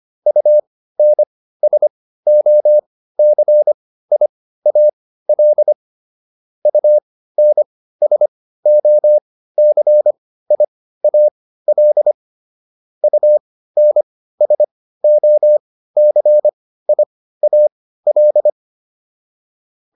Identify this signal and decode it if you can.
Morse code, heard as a keyed tone, about 25 wpm, spaced as for 15 wpm, 600 Hz. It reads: UNSOCIAL UNSOCIAL UNSOCIAL